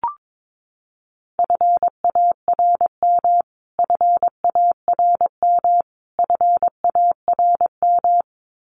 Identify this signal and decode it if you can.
Morse code, heard as a keyed tone, about 22 wpm, 700 Hz. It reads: FARM FARM FARM
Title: Top 500 Words Repeated 22wpm